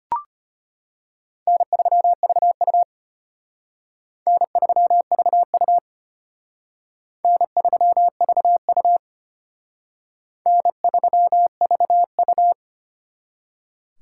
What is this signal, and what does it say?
Morse code, heard as a keyed tone, about 32 wpm, 700 Hz. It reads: N3VU N3VU N3VU N3VU